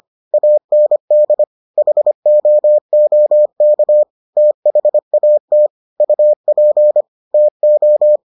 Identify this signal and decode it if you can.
Morse code, heard as a keyed tone, about 25 wpm, 600 Hz. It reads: AND HOOK THAT UP TO